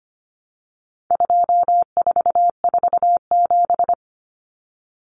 Morse code, heard as a keyed tone, about 25 wpm, 700 Hz. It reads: 2447
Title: Numbers - 4-Digit 25wpm